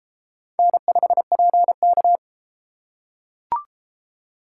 Morse code, heard as a keyed tone, about 31 wpm, 700 Hz. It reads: N5PK